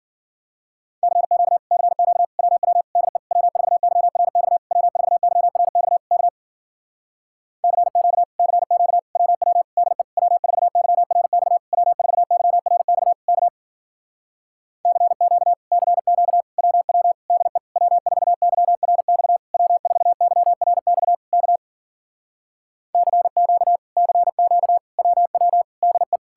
Morse code, heard as a keyed tone, about 54 wpm, 700 Hz. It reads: CQ CQ WW DE W4YRX W4YRX K CQ CQ WW DE W4YRX W4YRX K CQ CQ WW DE W4YRX W4YRX K CQ CQ WW DE